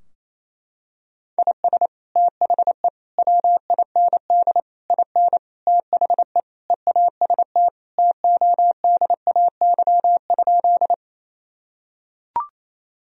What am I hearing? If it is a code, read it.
Morse code, heard as a keyed tone, about 28 wpm, 700 Hz. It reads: IS THE WIND IN THE EAST TODAY?
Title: Sentences from Top 1000 Words - Review Part 3 - 28wpm